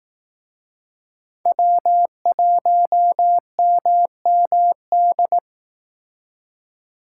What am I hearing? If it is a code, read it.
Morse code, heard as a keyed tone, about 18 wpm, 700 Hz. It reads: W1MMD
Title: Speed Racing - USA 1x3 Call Signs 15wpm